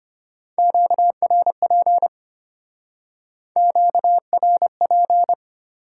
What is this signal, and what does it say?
Morse code, heard as a keyed tone, about 30 wpm, 700 Hz. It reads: QRP QRP
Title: Speed-Racing Top 90 QSO Elements 25wpm